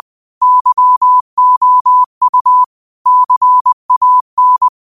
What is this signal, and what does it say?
Morse code, heard as a keyed tone, about 20 wpm, 1000 Hz. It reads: YOU CAN